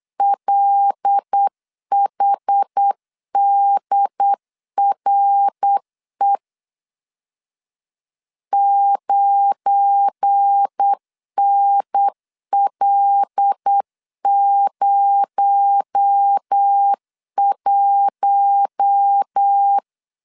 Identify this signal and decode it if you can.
Morse code, heard as a keyed tone, about 8 wpm, 800 Hz. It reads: LHDRE 9NL01
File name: LEZIONI / 7 GF9Z / 7 GF9 8wpm 3545